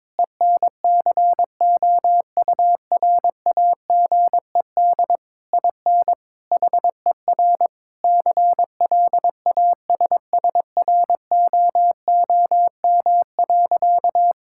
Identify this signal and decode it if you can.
Morse code, heard as a keyed tone, about 22 wpm, 700 Hz. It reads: ENCOURAGED IN HER CLASSROOM.
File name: Difficult Sentences with Hints - 22wpm